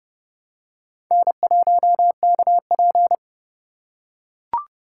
Morse code, heard as a keyed tone, about 31 wpm, 700 Hz. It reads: N1KP